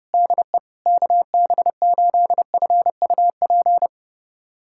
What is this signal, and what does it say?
Morse code, heard as a keyed tone, about 30 wpm, 700 Hz. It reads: DE KB8FUP
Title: USA Callsign DE Callsign 30wpm